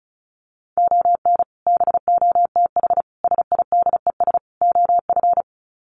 Morse code, heard as a keyed tone, about 34 wpm, 700 Hz. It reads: ON BOTH SIDES OF
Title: Sets of 4 Words 35wpm